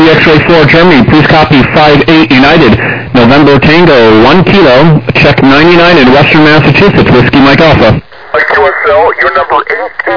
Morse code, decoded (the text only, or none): none